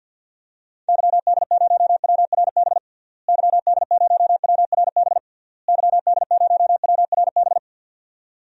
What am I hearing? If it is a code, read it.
Morse code, heard as a keyed tone, about 51 wpm, 700 Hz. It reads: YD0WRB YD0WRB YD0WRB
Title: DXCC-100 Call Signs - 6 Characters - Repeated 3x - 50wpm